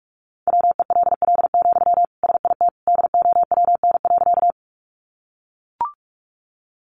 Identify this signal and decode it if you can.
Morse code, heard as a keyed tone, about 44 wpm, 700 Hz. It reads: WELL, SIT DOWN.
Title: Sentences from Top 500 Words 45wpm